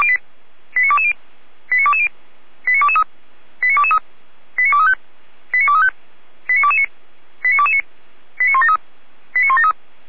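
Five-tone selective call (ZVEI), heard as quick runs of five tones, tones of about 70 ms. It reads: 89220 89220 89223 89223 89235 89235 89229 89229 89173 89173
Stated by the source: ZVEI five-tone call received by SDR.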